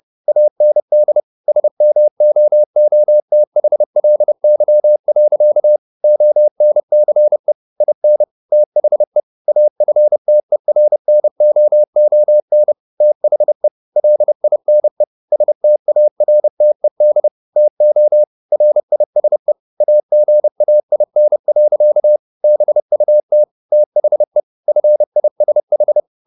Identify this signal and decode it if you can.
Morse code, heard as a keyed tone, about 30 wpm, 600 Hz. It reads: AND SMOOTHLY. ONCE IN THE AFTERNOON THE LINE STARTED TO RISE AGAIN. BUT THE FISH
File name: the old man and the sea / the old man and the sea 1 30wpm